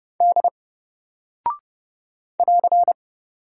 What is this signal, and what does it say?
Morse code, heard as a keyed tone, about 30 wpm, 700 Hz. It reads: D <AR>
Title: Single Character (Letter:Number:Punctuation:Prosign) Alt 30wpm